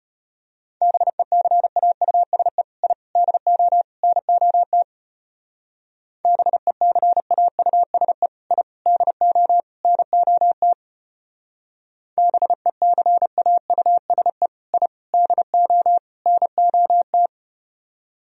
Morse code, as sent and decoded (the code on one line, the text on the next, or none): -... -.-. .- ..- ... / .. / -.. --- / -. --- - / -... -.-. .- ..- ... / .. / -.. --- / -. --- - / -... -.-. .- ..- ... / .. / -.. --- / -. --- -
BECAUSE I DO NOT BECAUSE I DO NOT BECAUSE I DO NOT